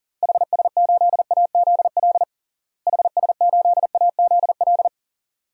Morse code, heard as a keyed tone, about 37 wpm, 700 Hz. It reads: HS8AZL HS8AZL
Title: DXCC-100 Call Signs - 6 Characters - Repeated 2x - 40wpm